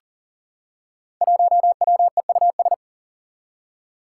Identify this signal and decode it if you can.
Morse code, heard as a keyed tone, about 41 wpm, 700 Hz. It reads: 1WEUS
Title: Five Challenging Letter-Number Combinations 40wpm